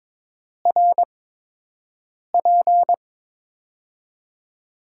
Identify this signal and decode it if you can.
Morse code, heard as a keyed tone, about 22 wpm, 700 Hz. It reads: R P
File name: Top 90 QSO Elements - 8x Character Spacing - 22wpm